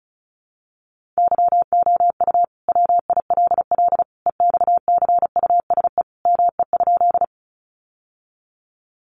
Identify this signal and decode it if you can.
Morse code, heard as a keyed tone, about 34 wpm, 700 Hz. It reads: YOU WILL EXCUSE ME?
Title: Sentences from Top 700 Words 35wpm